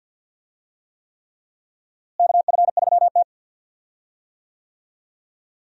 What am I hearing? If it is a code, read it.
Morse code, heard as a keyed tone, about 52 wpm, 700 Hz. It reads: KF3T
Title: USA Call Signs - 4 Characters - Repeated 3x - 50wpm